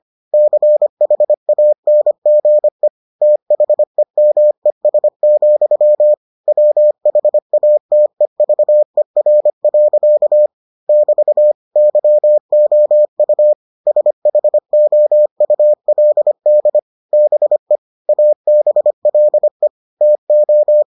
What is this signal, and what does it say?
Morse code, heard as a keyed tone, about 25 wpm, 600 Hz. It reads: CHANGE THEMES, WHATEVER. <BT> YOU SHOULD BE ABLE TO